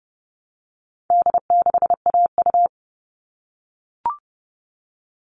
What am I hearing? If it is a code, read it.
Morse code, heard as a keyed tone, about 30 wpm, 700 Hz. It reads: D6AU
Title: All Cut Numbers - 4-Digit 30wpm